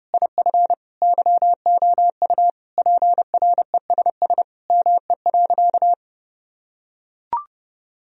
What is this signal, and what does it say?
Morse code, heard as a keyed tone, about 30 wpm, 700 Hz. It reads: IF YOU PRESS ME.
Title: Sentences from Top 1000 Words - Review Part 1 - 30wpm